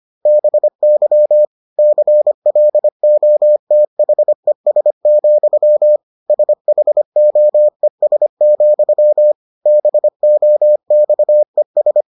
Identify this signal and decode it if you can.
Morse code, heard as a keyed tone, about 25 wpm, 600 Hz. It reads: BY CLOTHES, SHOES, BOXES